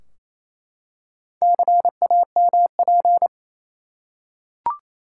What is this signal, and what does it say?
Morse code, heard as a keyed tone, about 28 wpm, 700 Hz. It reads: CAMP
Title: Four Letter Words - 28wpm